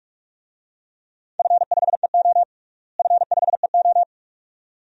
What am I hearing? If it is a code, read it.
Morse code, heard as a keyed tone, about 46 wpm, 700 Hz. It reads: F5EO F5EO
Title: DXCC-100 Call Signs - 4 Characters - Repeated 2x - 45wpm